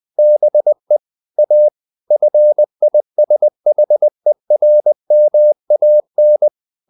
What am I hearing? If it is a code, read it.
Morse code, heard as a keyed tone, about 20 wpm, 600 Hz. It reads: BE A FISHERMAN
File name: the old man and the sea part 2 20wpm